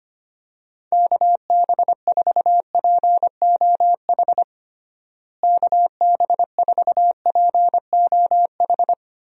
Morse code, heard as a keyed tone, about 25 wpm, 700 Hz. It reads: KB4POH KB4POH
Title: USA Call Signs - 6 Characters - Repeated 3x - 25wpm